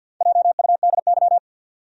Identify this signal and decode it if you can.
Morse code, heard as a keyed tone, about 52 wpm, 700 Hz. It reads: JUDY
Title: USA Common Names - Max 5 Letters - 50wpm